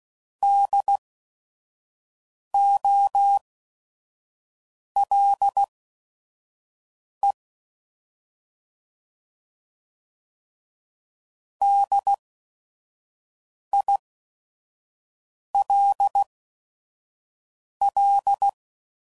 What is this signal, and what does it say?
Morse code, heard as a keyed tone, about 16 wpm, spaced as for 5 wpm, 800 Hz. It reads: DOLE DILL
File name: Prefixes / By First Letter / Ds